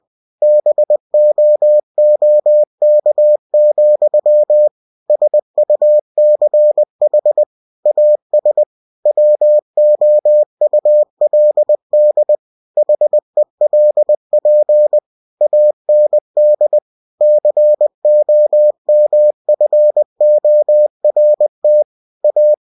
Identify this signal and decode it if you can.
Morse code, heard as a keyed tone, about 20 wpm, 600 Hz. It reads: BOOK, SUCH AS WOULD HELP AND COMFORT A
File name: cw-books / winnie-the-pooh / winnie-20wpm-page-0019